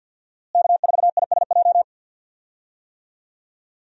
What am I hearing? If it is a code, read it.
Morse code, heard as a keyed tone, about 46 wpm, 700 Hz. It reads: K4ISJ
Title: USA Call Signs - 5 Characters - Repeated 2x - 50wpm